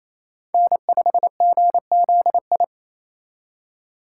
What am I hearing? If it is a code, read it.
Morse code, heard as a keyed tone, about 27 wpm, 700 Hz. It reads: N5GZI
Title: USA Call Signs - 5 Characters - 2x Character Spacing - 28wpm